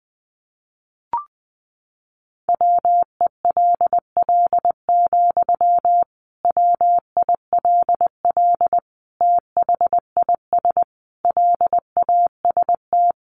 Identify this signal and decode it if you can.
Morse code, heard as a keyed tone, about 20 wpm, 700 Hz. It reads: WELL, WILL THIS LAST
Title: Sentences from Top 1300 Words 20wpm